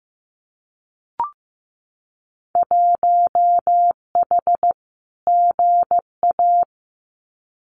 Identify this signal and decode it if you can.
Morse code, heard as a keyed tone, about 15 wpm, 700 Hz. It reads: 1H GA
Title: Winter Field Day Exchanges 15wpm